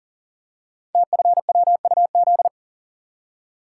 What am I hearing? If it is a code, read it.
Morse code, heard as a keyed tone, about 38 wpm, 700 Hz. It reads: TFWUZ
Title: Code Groups - 5 Letter:Numbers - 40wpm